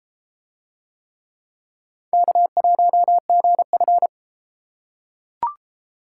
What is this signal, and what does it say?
Morse code, heard as a keyed tone, about 31 wpm, 700 Hz. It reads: K1GF